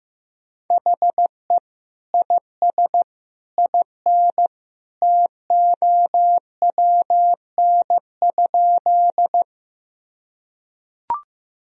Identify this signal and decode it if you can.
Morse code, heard as a keyed tone, about 15 wpm, 700 Hz. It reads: HE IS IN TOWN?
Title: Sentences from Top 1000 Words - Review Part 2 - 15wpm